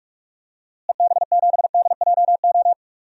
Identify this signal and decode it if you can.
Morse code, heard as a keyed tone, about 46 wpm, 700 Hz. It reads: EB7DJO
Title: DXCC-100 Call Signs - 6 Characters - Repeated 2x - 45wpm